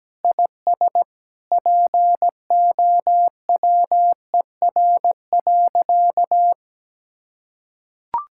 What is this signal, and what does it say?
Morse code, heard as a keyed tone, about 17 wpm, 700 Hz. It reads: IS POWER.
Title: Sentences from Top 1000 Words - Review Part 5 - 17wpm